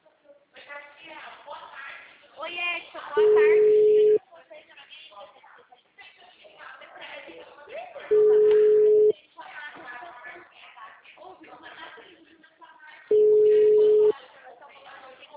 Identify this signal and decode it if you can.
Ringback tone (Europe, 425 Hz), with speech.